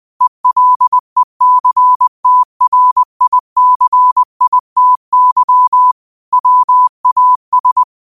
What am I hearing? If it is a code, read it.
Morse code, heard as a keyed tone, about 20 wpm, 1000 Hz. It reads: ELECTRICITY WAS
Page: New quotes every day in morse code at 20 Words per minute.